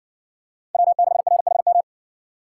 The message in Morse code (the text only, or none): W6RHM